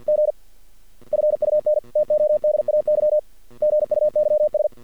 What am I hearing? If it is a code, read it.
Morse code, heard as a keyed tone, about 48 wpm, 600 Hz. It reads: G FUN TODAY GUYS
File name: archived / CRSnet